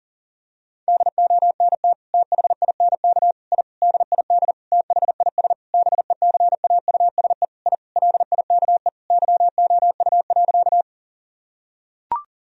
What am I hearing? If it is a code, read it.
Morse code, heard as a keyed tone, about 40 wpm, 700 Hz. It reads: DONT THINK I DID THIS BECAUSE I LIKE YOU.
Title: Sentences from Top 1000 Words - Review Part 3 - 40wpm